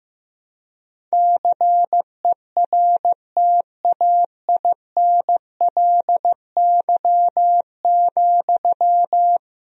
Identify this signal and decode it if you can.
Morse code, heard as a keyed tone, about 15 wpm, 700 Hz. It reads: CERTAINLY,
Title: Sentences from Top 1000 Words - Review Part 1 - 15wpm